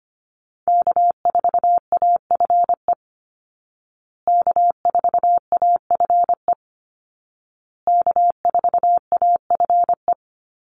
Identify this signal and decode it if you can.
Morse code, heard as a keyed tone, about 25 wpm, 700 Hz. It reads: K4AFE K4AFE K4AFE
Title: Call Signs Repeated 2x - 25wpm